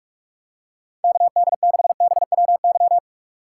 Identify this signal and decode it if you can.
Morse code, heard as a keyed tone, about 46 wpm, 700 Hz. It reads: KD6BWY
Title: USA Call Signs - 6x Character Spacing - 45wpm